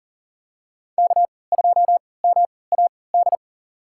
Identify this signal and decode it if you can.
Morse code, heard as a keyed tone, about 40 wpm, spaced as for 23 wpm, 700 Hz. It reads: K2MAD
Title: USA Call Signs - 5 Characters - 3x Character Spacing - 40wpm